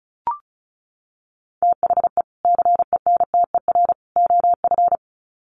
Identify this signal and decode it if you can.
Morse code, heard as a keyed tone, about 34 wpm, 700 Hz. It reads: THE CENTER OF
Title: Sets of 3 Words 35wpm